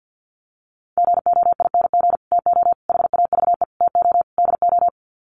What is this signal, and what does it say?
Morse code, heard as a keyed tone, about 49 wpm, 700 Hz. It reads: GOING TO HAVE TO DO
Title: Sets of 5 Words Encore 50wpm